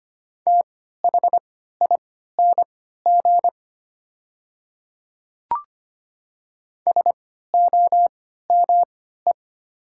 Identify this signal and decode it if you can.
Morse code, heard as a keyed tone, about 25 wpm, spaced as for 14 wpm, 700 Hz. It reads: THING SOME
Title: 3x Character Spacing - Top 200 Words - 25wpm